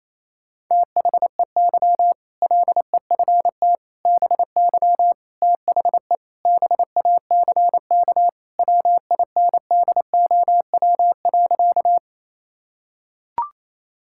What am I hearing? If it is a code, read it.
Morse code, heard as a keyed tone, about 28 wpm, 700 Hz. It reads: THEY LEFT BY THE BACK WINDOW.
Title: Sentences from Top 1600 Words 28wpm